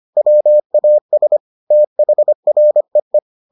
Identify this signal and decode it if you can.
Morse code, heard as a keyed tone, about 25 wpm, 600 Hz. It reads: WAS THREE